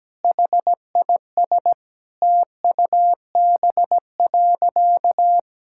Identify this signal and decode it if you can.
Morse code, heard as a keyed tone, about 17 wpm, 700 Hz. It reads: HIS TUB.
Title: Pangram Challenge v2 - 17wpm